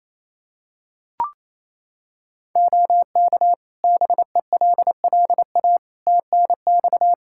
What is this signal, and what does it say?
Morse code, heard as a keyed tone, about 27 wpm, 700 Hz. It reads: OK BELLA TNX
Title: K1USN SST QSO Practice 28wpm